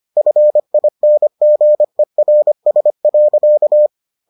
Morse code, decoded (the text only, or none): FINGERS.